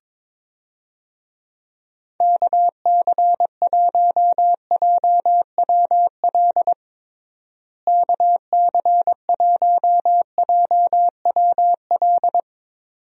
Morse code, heard as a keyed tone, about 22 wpm, 700 Hz. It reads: KC1JWL KC1JWL
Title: USA Call Signs - 6 Characters - Repeated 2x - 22wpm